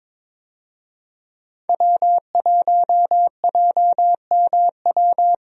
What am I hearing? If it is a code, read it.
Morse code, heard as a keyed tone, about 22 wpm, 700 Hz. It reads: W1JMW
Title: USA Call Signs - 5 Characters - Repeated 3x - 22wpm